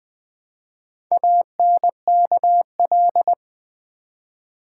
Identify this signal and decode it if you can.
Morse code, heard as a keyed tone, about 21 wpm, 690 Hz. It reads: ANKL
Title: Build A Word - 1 Letter At A Time - 5-Letter Words - 20wpm